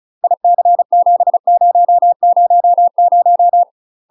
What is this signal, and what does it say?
Morse code, heard as a keyed tone, about 36 wpm, 700 Hz. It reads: IC7000
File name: UK MASTER QSOs TRAINING / 35WPM / UK MASTER QSOs 0031